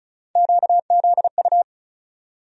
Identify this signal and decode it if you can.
Morse code, heard as a keyed tone, about 36 wpm, 700 Hz. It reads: QZU
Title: USA Call Sign Postfixes - 3 Letters 35wpm